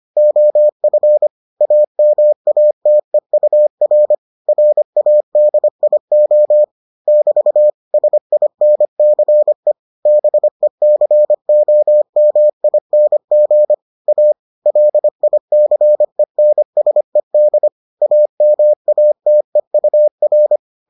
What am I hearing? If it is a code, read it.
Morse code, heard as a keyed tone, about 25 wpm, 600 Hz. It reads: OF AMATEUR RADIO <BT> SINCE BECOMING A LICENSED AMATEUR